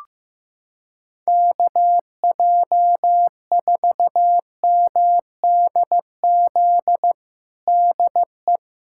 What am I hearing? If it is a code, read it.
Morse code, heard as a keyed tone, about 15 wpm, 700 Hz. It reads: KJ4MDZ DE
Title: USA Callsign DE Callsign - 6 Characters - 15wpm